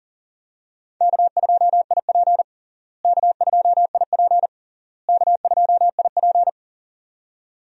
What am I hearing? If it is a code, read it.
Morse code, heard as a keyed tone, about 40 wpm, 700 Hz. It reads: K2IP K2IP K2IP
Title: USA Call Signs - 4 Characters - Repeated 3x - 40wpm